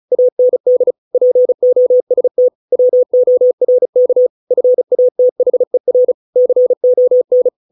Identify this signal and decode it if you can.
Morse code, heard as a keyed tone, about 34 wpm, 500 Hz. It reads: AND POST WORK FATHER CON